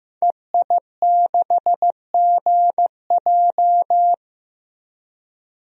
Morse code, heard as a keyed tone, about 15 wpm, 700 Hz. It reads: EI6GJ
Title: DXCC-100 Call Signs - 5 Characters - 15wpm